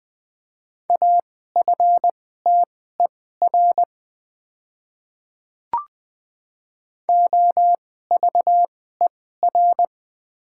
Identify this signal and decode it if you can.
Morse code, heard as a keyed tone, about 21 wpm, spaced as for 15 wpm, 700 Hz. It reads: AFTER OVER
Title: 2x Character Spacing - Top 200 Words - 20wpm